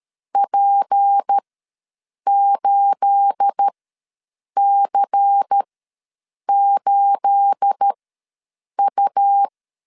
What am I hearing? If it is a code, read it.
Morse code, heard as a keyed tone, about 13 wpm, spaced as for 7 wpm, 800 Hz. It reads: P8C8U
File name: LEZIONI / 9 28 ws 12wpm 3354